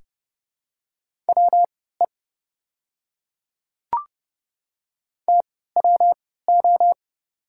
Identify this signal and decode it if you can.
Morse code, heard as a keyed tone, about 30 wpm, spaced as for 17 wpm, 700 Hz. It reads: WE TWO
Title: 3x Character Spacing - Top 200 Words - 30wpm